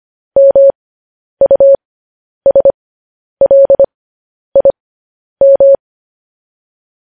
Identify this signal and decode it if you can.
Morse code, heard as a keyed tone, about 25 wpm, spaced as for 10 wpm, 550 Hz. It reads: MUSLIM
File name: podcast / News headlines-1x-25x10